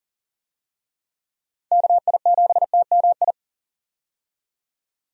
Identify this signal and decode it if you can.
Morse code, heard as a keyed tone, about 40 wpm, 700 Hz. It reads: KI7TMI
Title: USA Call Signs - 6 Characters - 40wpm